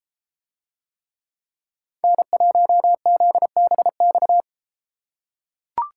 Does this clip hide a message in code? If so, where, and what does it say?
Yes: Morse code from 2.0 to 4.4 s, heard as a keyed tone, about 32 wpm, 700 Hz. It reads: N1ZBX